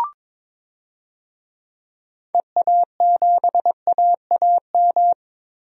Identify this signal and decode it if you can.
Morse code, heard as a keyed tone, about 22 wpm, 700 Hz. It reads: EA7AAM
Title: DXCC-100 Call Signs - 6 Characters - 22wpm